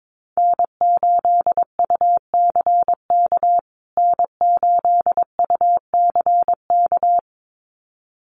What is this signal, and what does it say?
Morse code, heard as a keyed tone, about 22 wpm, 700 Hz. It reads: N8UCK N8UCK
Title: USA Call Signs Repeated 2x 22wpm